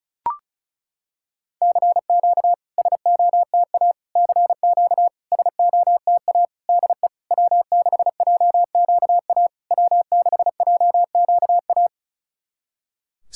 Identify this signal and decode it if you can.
Morse code, heard as a keyed tone, about 36 wpm, 700 Hz. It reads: CQ SOTA CQ SOTA DE W6JQA W6JQA
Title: SOTA QSO Practice 35wpm